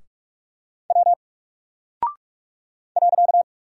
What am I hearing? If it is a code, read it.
Morse code, heard as a keyed tone, about 45 wpm, 700 Hz. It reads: W .
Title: Single Character (Letter:Number:Punctuation:Prosign) 45wpm